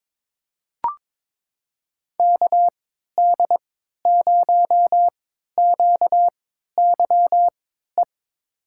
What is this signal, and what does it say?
Morse code, heard as a keyed tone, about 21 wpm, spaced as for 12 wpm, 700 Hz. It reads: KD0QYE
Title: USA Call Signs - 6 Characters - 3x Character Spacing - 22wpm